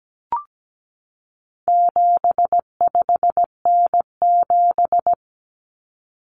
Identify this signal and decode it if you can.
Morse code, heard as a keyed tone, about 17 wpm, 700 Hz. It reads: 75N7
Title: Common Cut Numbers - 4-Digit 17wpm